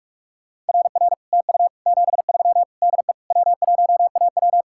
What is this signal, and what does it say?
Morse code, heard as a keyed tone, about 52 wpm, 700 Hz. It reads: RR TU 73 DE W1AW
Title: Speed-Racing Common QSO Phrases Encore - 45wpm